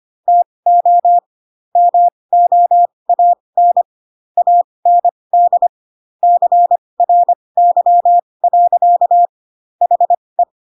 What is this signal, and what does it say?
Morse code, heard as a keyed tone, about 25 wpm, spaced as for 20 wpm, 700 Hz. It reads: TO MOAN AND CRY. HE
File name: Pinocchio / Pinocchio ch1 15 25 ik1hge